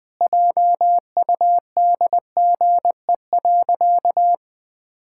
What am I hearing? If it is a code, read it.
Morse code, heard as a keyed tone, about 20 wpm, 700 Hz. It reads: JUDGE.
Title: Pangram Challenge v2 - 20wpm